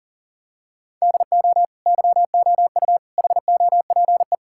Morse code, heard as a keyed tone, about 37 wpm, 690 Hz. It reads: DO YOU HOPE
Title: Sentences from Top 1000 Words - Review Part 6 - 40wpm